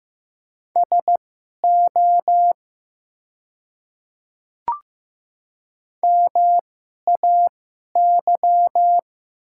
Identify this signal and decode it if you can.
Morse code, heard as a keyed tone, about 15 wpm, spaced as for 11 wpm, 700 Hz. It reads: SO MAY
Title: 2x Character Spacing - Top 200 Words - 15wpm